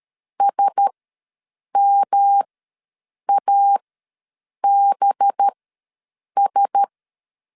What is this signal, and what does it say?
Morse code, heard as a keyed tone, about 13 wpm, spaced as for 7 wpm, 800 Hz. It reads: SMABS
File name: LEZIONI / 5 LBVC / 5 LB ws 12wpm 3034